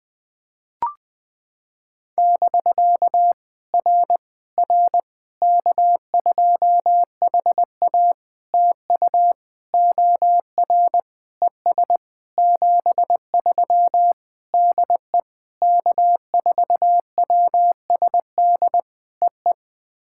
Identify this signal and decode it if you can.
Morse code, heard as a keyed tone, about 18 wpm, 700 Hz. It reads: <BK> R R K2HA TU OR ES 73 DE K4WSD EE
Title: POTA QSO Practice - 20wpm